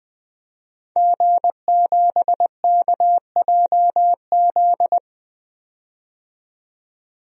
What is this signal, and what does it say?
Morse code, heard as a keyed tone, about 21 wpm, 700 Hz. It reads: G7KJZ
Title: DXCC-100 Call Signs - 5 Characters - 20wpm